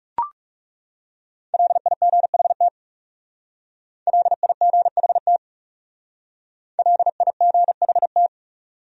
Morse code, heard as a keyed tone, about 39 wpm, 700 Hz. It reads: LIGHT LIGHT LIGHT